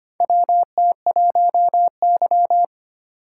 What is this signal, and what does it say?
Morse code, heard as a keyed tone, about 25 wpm, 700 Hz. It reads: WT1Y
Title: USA Call Signs - 8x Character Spacing - 25wpm